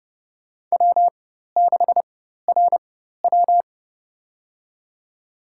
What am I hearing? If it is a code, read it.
Morse code, heard as a keyed tone, about 30 wpm, spaced as for 14 wpm, 700 Hz. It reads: W6RW
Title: USA Call Signs - 4 Characters - 4x Character Spacing - 30wpm